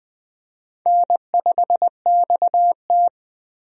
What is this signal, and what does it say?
Morse code, heard as a keyed tone, about 20 wpm, 700 Hz. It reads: N5XT
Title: USA Call Signs - 4 Characters - 20wpm